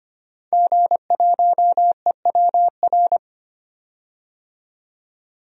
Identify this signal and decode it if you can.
Morse code, heard as a keyed tone, about 25 wpm, 700 Hz. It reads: G1EWR
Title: DXCC-100 Call Signs - 5 Characters - 25wpm